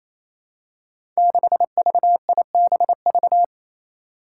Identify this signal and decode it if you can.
Morse code, heard as a keyed tone, about 28 wpm, 700 Hz. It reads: 6VIBV
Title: Five Challenging Letter-Number Combinations 28wpm